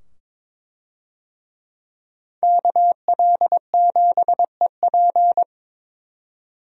Morse code, heard as a keyed tone, about 22 wpm, 700 Hz. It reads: KL7EP
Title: DXCC-100 Call Signs - 5 Characters - 22wpm